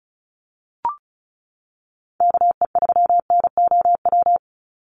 Morse code, heard as a keyed tone, about 34 wpm, 700 Hz. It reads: KE3NOW
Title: USA Call Signs 35wpm